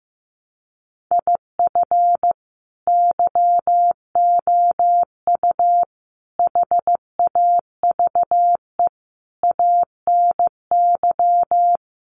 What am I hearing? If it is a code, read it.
Morse code, heard as a keyed tone, about 15 wpm, 700 Hz. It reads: IF YOU HAVE ANY
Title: Sentences from Top 1100 Words 15wpm